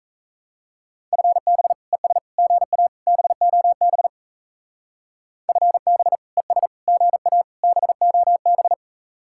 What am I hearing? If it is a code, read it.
Morse code, heard as a keyed tone, about 34 wpm, 700 Hz. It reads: FB ES GA BOB FB ES GA BOB
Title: Speed-Racing Common QSO Phrases - 28wpm